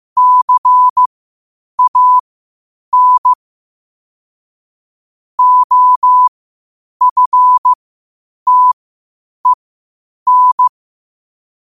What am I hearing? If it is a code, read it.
Morse code, heard as a keyed tone, about 15 wpm, spaced as for 8 wpm, 1000 Hz. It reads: CAN OFTEN
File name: qotdfiles / 8wpm